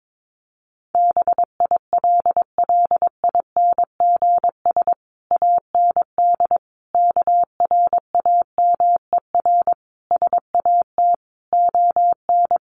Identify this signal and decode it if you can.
Morse code, heard as a keyed tone, about 22 wpm, 700 Hz. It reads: BILLINGS AND KRAMER SAT ON
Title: A Sound Of Thunder 22wpm